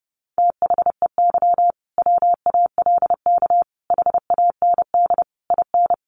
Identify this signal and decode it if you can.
Morse code, heard as a keyed tone, about 30 wpm, 700 Hz. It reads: THEY WALK HAND IN
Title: Sentences from Top 1400 Words 30wpm